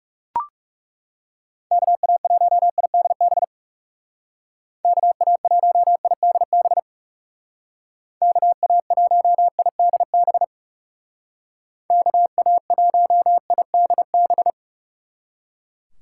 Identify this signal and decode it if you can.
Morse code, heard as a keyed tone, about 39 wpm, 700 Hz. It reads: KA1IDB KA1IDB KA1IDB KA1IDB